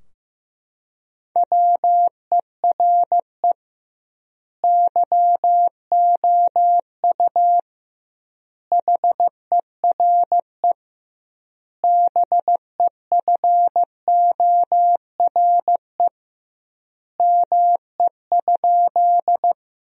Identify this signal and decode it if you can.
Morse code, heard as a keyed tone, about 16 wpm, 700 Hz. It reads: WERE YOU HERE BEFORE ME?
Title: 2x Word-Spacing - Sentences from Top 200 Words - 15wpm